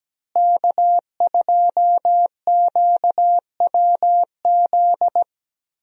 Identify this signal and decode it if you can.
Morse code, heard as a keyed tone, about 17 wpm, 700 Hz. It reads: K2QWZ
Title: USA Call Signs - 5 Characters - Repeated 2x - 17wpm